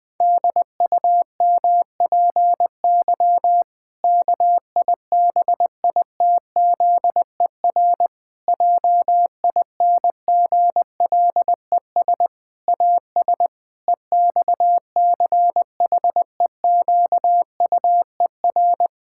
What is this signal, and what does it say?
Morse code, heard as a keyed tone, about 21 wpm, 700 Hz. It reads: DUMPY KIBITZER JINGLES AS EXCHEQUER
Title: Pangram Challenge v2 - 20wpm